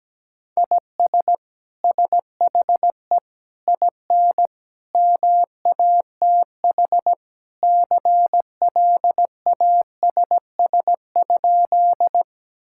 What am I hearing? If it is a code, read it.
Morse code, heard as a keyed tone, about 17 wpm, 700 Hz. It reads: IS SHE IN MATH CLASS?
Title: Sentences from Top 1000 Words - Review Part 6 - 17wpm